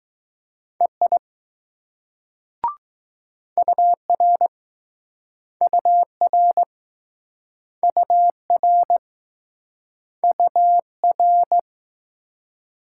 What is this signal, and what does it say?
Morse code, heard as a keyed tone, about 20 wpm, 700 Hz. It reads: EI UR UR UR UR